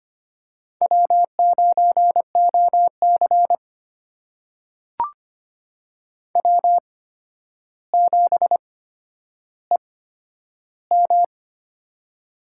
Morse code, heard as a keyed tone, about 25 wpm, spaced as for 11 wpm, 700 Hz. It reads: W9OC W7EM
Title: USA Call Signs - 4 Characters - 8x Character Spacing - 25wpm